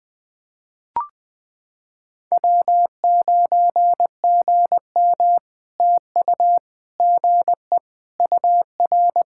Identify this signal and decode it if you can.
Morse code, heard as a keyed tone, about 20 wpm, 700 Hz. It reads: W9GM TU GE UR
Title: POTA QSO Practice - 20wpm